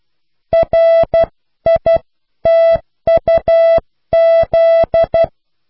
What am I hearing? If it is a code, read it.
Morse code, heard as a keyed tone, about 12 wpm, 660 Hz. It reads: RITUZ